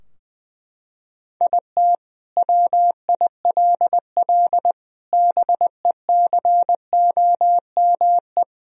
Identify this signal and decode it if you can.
Morse code, heard as a keyed tone, about 20 wpm, 700 Hz. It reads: IT WILL BECOME
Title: Sentences from Top 1000 Words - Review Part 3 - 20wpm